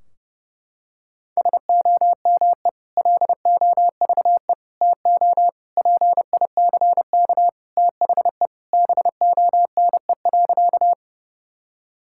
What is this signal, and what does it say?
Morse code, heard as a keyed tone, about 32 wpm, 700 Hz. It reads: SOME LOVE TO PICK THE BONE.
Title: Sentences from Top 1600 Words 30wpm